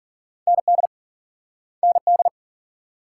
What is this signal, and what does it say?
Morse code, heard as a keyed tone, about 49 wpm, 700 Hz. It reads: ND ND
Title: Speed-Racing Multi-letter Phonemes - 40wpm